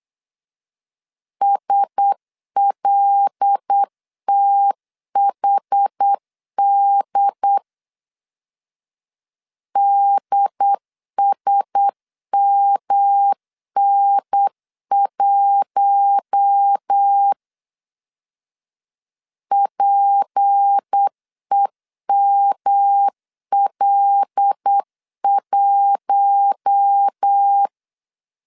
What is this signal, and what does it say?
Morse code, heard as a keyed tone, about 8 wpm, 800 Hz. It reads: SLTHD DSMN1 PEML1
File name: LEZIONI / 6 PKX0 / 6 P 8wpm 3025